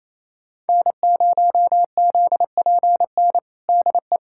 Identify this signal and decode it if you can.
Morse code, heard as a keyed tone, about 28 wpm, 700 Hz. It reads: N0ZPN DE
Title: USA Callsign DE Callsign 28wpm